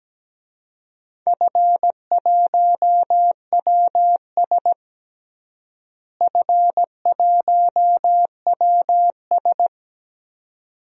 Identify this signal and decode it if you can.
Morse code, heard as a keyed tone, about 17 wpm, 700 Hz. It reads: F1WS F1WS
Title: DXCC-100 Call Signs - 4 Characters - Repeated 2x - 17wpm